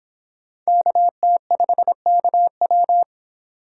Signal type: Morse code

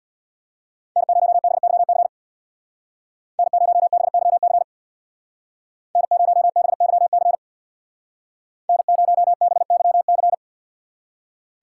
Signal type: Morse code

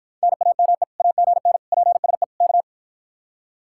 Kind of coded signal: Morse code

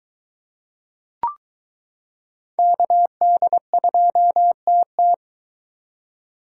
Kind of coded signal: Morse code

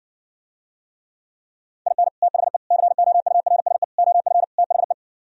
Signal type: Morse code

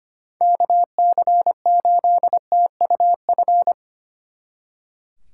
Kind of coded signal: Morse code

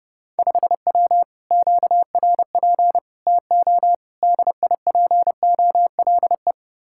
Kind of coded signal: Morse code

